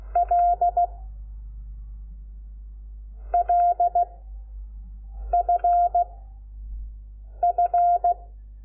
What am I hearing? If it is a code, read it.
Morse code, heard as a keyed tone, about 16 wpm, 700 Hz. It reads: L L F F
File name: Lecciones / CLASE Nº5